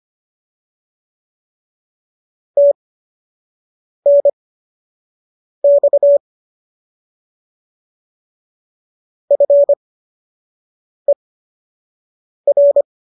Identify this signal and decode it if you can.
Morse code, heard as a keyed tone, about 25 wpm, spaced as for 6 wpm, 570 Hz. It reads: TNX FER
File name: qso002 06